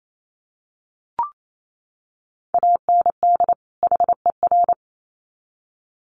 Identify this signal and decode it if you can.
Morse code, heard as a keyed tone, about 28 wpm, 700 Hz. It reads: AND HER
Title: Sets of 2 Words 28wpm